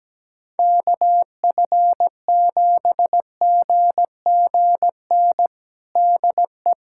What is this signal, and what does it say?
Morse code, heard as a keyed tone, about 17 wpm, 700 Hz. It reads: KF7GGN DE
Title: USA Callsign DE Callsign 17wpm